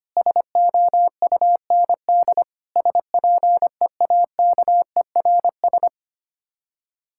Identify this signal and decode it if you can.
Morse code, heard as a keyed tone, about 25 wpm, 700 Hz. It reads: SOUND SPEAKERS
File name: practice / 25wpm / Difficult Words - Speed-Racing with Hints - 25wpm